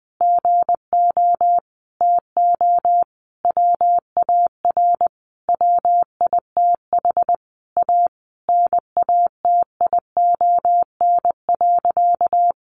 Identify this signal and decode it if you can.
Morse code, heard as a keyed tone, about 20 wpm, 700 Hz. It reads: GO TO WAR WITH A NATION.
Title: Sentences from Top 1200 Words 20wpm